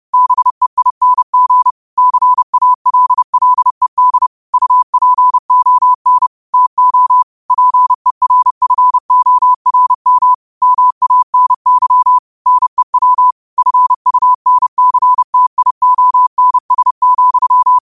Morse code, heard as a keyed tone, about 30 wpm, 1000 Hz. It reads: BEING CALLED UPON TO PERFORM MANY NEW FUNCTIONS,
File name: qotdfiles / 30wpm